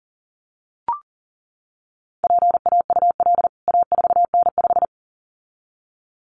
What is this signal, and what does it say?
Morse code, heard as a keyed tone, about 40 wpm, 700 Hz. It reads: PAUL A4N5
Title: CWT Exchange 40wpm